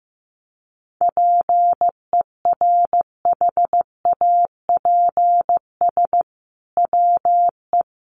Morse code, heard as a keyed tone, about 15 wpm, 700 Hz. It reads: PERHAPS WE
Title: Sentences from Top 1500 Words 15wpm